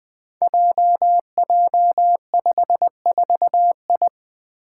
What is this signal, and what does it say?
Morse code, heard as a keyed tone, about 20 wpm, 700 Hz. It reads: JJ54I
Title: Five Challenging Letter-Number Combinations 20wpm